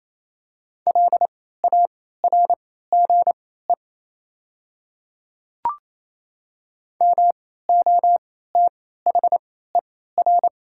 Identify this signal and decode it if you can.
Morse code, heard as a keyed tone, about 28 wpm, spaced as for 16 wpm, 690 Hz. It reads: LARGE MOTHER
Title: 3x Character Spacing - Top 200 Words - 28wpm